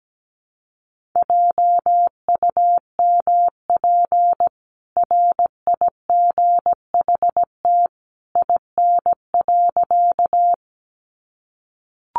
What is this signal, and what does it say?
Morse code, heard as a keyed tone, about 17 wpm, 700 Hz. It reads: JUMP RIGHT IN.
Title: Sentences from Top 1300 Words 17wpm